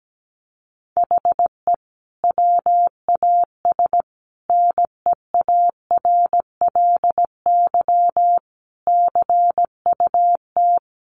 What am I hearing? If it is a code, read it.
Morse code, heard as a keyed tone, about 17 wpm, 700 Hz. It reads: HE WAS NEARLY CUT
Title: Sentences from Top 1300 Words 17wpm